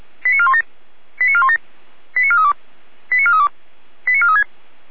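Five-tone selective call (ZVEI), heard as quick runs of five tones, tones of about 70 ms. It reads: 89417 89417 89432 89432 89436
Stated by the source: ZVEI five-tone call received via SDR.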